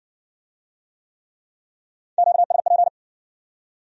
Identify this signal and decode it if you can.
Morse code, heard as a keyed tone, about 59 wpm, 700 Hz. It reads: QSP